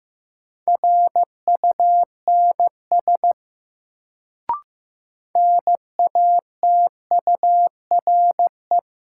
Morse code, heard as a keyed tone, about 15 wpm, 700 Hz. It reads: RUNS NATURE
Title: Sentences from Top 600 Words - New Word Drill - 15wpm